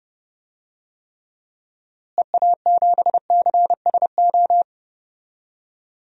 Morse code, heard as a keyed tone, about 31 wpm, 700 Hz. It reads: EA7CSO
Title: DXCC-100 Call Signs - 6 Characters - Repeated 3x - 30wpm